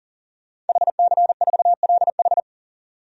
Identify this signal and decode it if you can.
Morse code, heard as a keyed tone, about 40 wpm, 700 Hz. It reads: HC4LH
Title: DXCC-100 Call Signs - 5 Characters - 40wpm